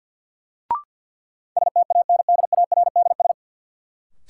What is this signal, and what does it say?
Morse code, heard as a keyed tone, about 52 wpm, 700 Hz. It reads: STANDARDS
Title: Sentences from Top 1000 Words - Review Part 6 - New Word Drill - 50wpm